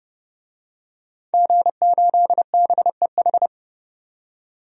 Morse code, heard as a keyed tone, about 32 wpm, 700 Hz. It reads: G8BEH
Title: DXCC-100 Call Signs - 5 Characters - 30wpm